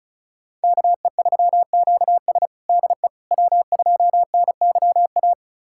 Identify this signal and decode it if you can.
Morse code, heard as a keyed tone, about 34 wpm, 700 Hz. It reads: KE3QS DE W2NYA
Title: USA Callsign DE Callsign - 5 Characters - 35wpm